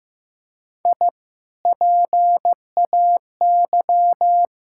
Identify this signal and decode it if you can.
Morse code, heard as a keyed tone, about 15 wpm, 700 Hz. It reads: I PAY
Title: Sentences from Top 1000 Words - Review Part 5 - 15wpm